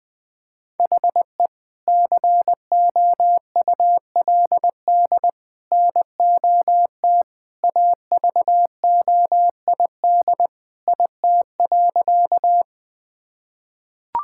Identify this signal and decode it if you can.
Morse code, heard as a keyed tone, about 20 wpm, 700 Hz. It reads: HE COULD NOT AVOID IT.
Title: Sentences from Top 1000 Words - Review Part 6 - 20wpm